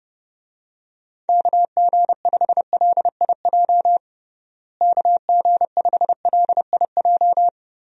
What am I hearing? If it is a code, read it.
Morse code, heard as a keyed tone, about 32 wpm, 700 Hz. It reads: KG5LIJ KG5LIJ
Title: USA Call Signs - 6 Characters - Repeated 3x - 30wpm